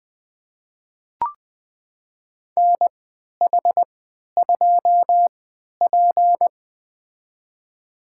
Morse code, heard as a keyed tone, about 20 wpm, spaced as for 11 wpm, 700 Hz. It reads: NH2P
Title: USA Call Signs - 4 Characters - 3x Character Spacing - 20wpm